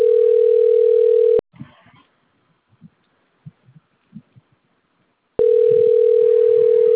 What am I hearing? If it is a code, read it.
Ringback tone (North America), 440 + 480 Hz.